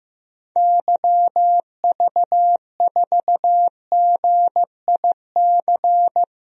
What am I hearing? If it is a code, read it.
Morse code, heard as a keyed tone, about 15 wpm, 700 Hz. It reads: YV4GIC
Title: DXCC-100 Call Signs - 6 Characters - Repeated 3x - 15wpm